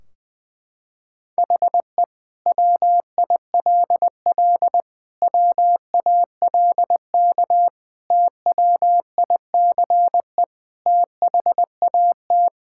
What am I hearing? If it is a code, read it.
Morse code, heard as a keyed tone, about 20 wpm, 700 Hz. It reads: HE WILL WALK TWICE THAT
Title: Sentences from Top 1600 Words 20wpm